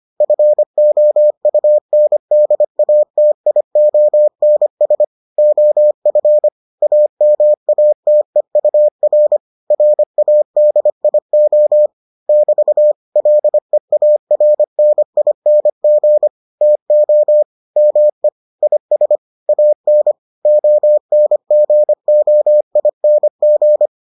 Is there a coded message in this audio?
Yes: Morse code from 0.2 to 23.9 s, heard as a keyed tone, about 25 wpm, 600 Hz. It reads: FOUNDATIONS OF AMATEUR RADIO <BT> LEARNING TO ME IS AN ONGOING